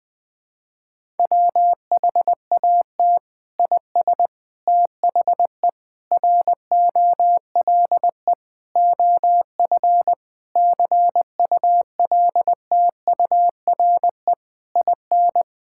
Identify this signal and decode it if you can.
Morse code, heard as a keyed tone, about 21 wpm, 700 Hz. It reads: WHAT IS THE ROLE OF CULTURE IN
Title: Sentences from Top 1000 Words - Review Part 6 - 20wpm